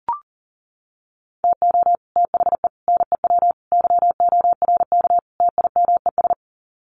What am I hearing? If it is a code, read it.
Morse code, heard as a keyed tone, about 40 wpm, 700 Hz. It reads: TO THE NEW YORK TIMES
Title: Sets of 5 Words Encore2 40wpm